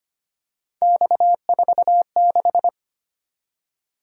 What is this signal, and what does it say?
Morse code, heard as a keyed tone, about 25 wpm, 700 Hz. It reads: X46
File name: Three Letter-Number Combinations - 25wpm